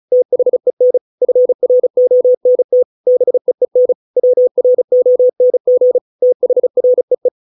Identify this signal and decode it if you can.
Morse code, heard as a keyed tone, about 34 wpm, 500 Hz. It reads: THEN FRONT BEEN WRONG THREE